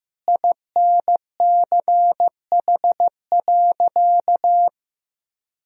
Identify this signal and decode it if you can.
Morse code, heard as a keyed tone, about 15 wpm, 700 Hz. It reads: INCH.
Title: Sentences from Top 1000 Words - Review Part 3 - 15wpm